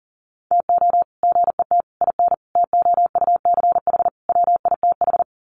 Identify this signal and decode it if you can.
Morse code, heard as a keyed tone, about 40 wpm, 700 Hz. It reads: TO GET IN TOUCH WITH
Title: Sets of 5 Words Encore2 40wpm